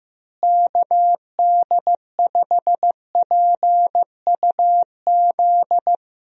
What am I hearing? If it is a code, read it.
Morse code, heard as a keyed tone, about 15 wpm, 700 Hz. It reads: KD5PUZ
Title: USA Call Signs - 6 Characters - Repeated 2x - 15wpm